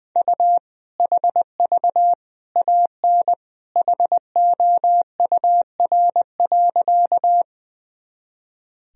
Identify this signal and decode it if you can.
Morse code, heard as a keyed tone, about 19 wpm, 700 Hz. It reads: U HV AN HOUR.
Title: Sentences from Top 400 Words and Abbreviations 20wpm